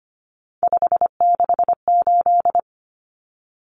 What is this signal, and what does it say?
Morse code, heard as a keyed tone, about 25 wpm, 700 Hz. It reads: 568
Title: Common Cut Numbers - 3-Digit 25wpm